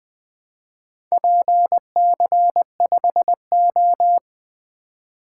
Morse code, heard as a keyed tone, about 20 wpm, 700 Hz. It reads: PC5O
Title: Four Letter-Number Combinations - 20wpm